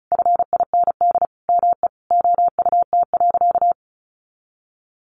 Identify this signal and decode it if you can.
Morse code, heard as a keyed tone, about 34 wpm, 700 Hz. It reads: FIND ME OUT.
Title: Sentences from Top 500 Words 35wpm